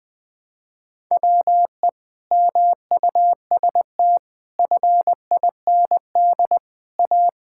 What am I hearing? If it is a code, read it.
Morse code, heard as a keyed tone, about 20 wpm, 700 Hz. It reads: WE MUST FIND A
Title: Sentences with 4-Word Phrases - 20wpm